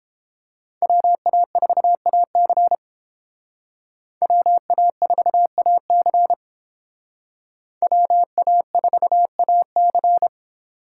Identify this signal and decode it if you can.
Morse code, heard as a keyed tone, about 31 wpm, 700 Hz. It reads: WA4AC WA4AC WA4AC